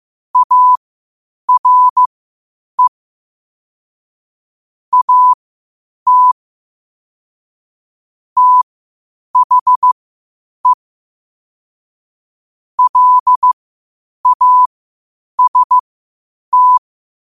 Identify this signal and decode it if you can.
Morse code, heard as a keyed tone, about 15 wpm, spaced as for 8 wpm, 1000 Hz. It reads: ARE AT THE LAST
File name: qotdfiles / 8wpm